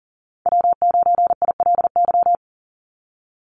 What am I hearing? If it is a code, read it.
Morse code, heard as a keyed tone, about 38 wpm, 700 Hz. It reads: W9ILY
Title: Call Signs - 40wpm